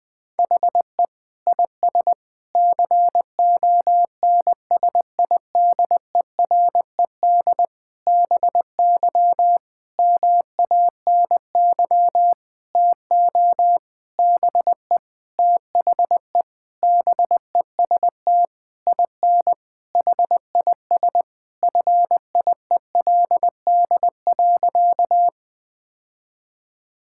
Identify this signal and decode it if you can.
Morse code, heard as a keyed tone, about 21 wpm, 700 Hz. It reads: HE IS CONSIDERED BY MANY TO BE THE BEST IN HIS FIELD.
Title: Sentences with 7-Word Phrases - Spoken Beginning - 20wpm